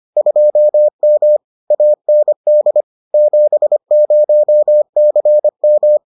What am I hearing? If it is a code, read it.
Morse code, heard as a keyed tone, about 25 wpm, 600 Hz. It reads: 2M AND 70CM